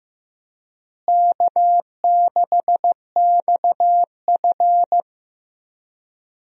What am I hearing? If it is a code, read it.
Morse code, heard as a keyed tone, about 15 wpm, 700 Hz. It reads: K6XF
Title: USA Call Signs - 4 Characters - Repeated 2x - 15wpm